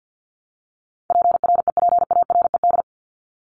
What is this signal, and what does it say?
Morse code, heard as a keyed tone, about 50 wpm, 700 Hz. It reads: PREPARED
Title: ED Suffix - Top 100 Words with Base Words 50wpm